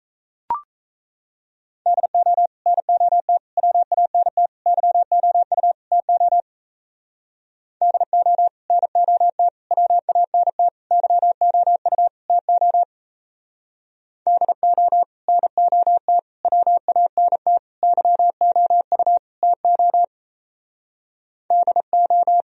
Morse code, heard as a keyed tone, about 34 wpm, 700 Hz. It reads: DO NOT WANT YOU TO DO NOT WANT YOU TO DO NOT WANT YOU TO DO
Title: Speed-Racing - Sets of 5 Words 28wpm